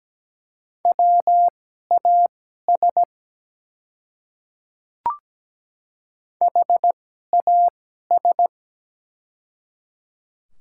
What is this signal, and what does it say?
Morse code, heard as a keyed tone, about 17 wpm, spaced as for 12 wpm, 690 Hz. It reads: WAS HAS
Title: 2x Character Spacing - Top 100 Words - 17wpm